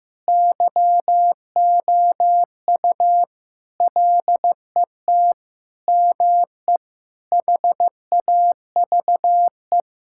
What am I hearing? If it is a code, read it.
Morse code, heard as a keyed tone, about 15 wpm, 700 Hz. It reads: YOU LET ME HAVE
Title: Sentences from Top 1600 Words 15wpm